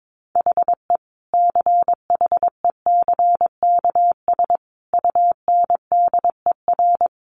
Morse code, heard as a keyed tone, about 22 wpm, 700 Hz. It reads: HE CHECKS UNDER
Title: Easy Phrases 22wpm